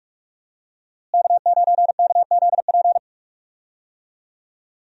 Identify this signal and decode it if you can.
Morse code, heard as a keyed tone, about 46 wpm, 700 Hz. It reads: K9KZP
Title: USA Call Signs - 5 Characters - Repeated 3x - 45wpm